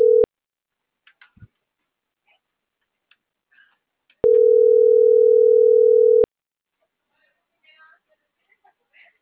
Ringback tone (North America), 440 + 480 Hz.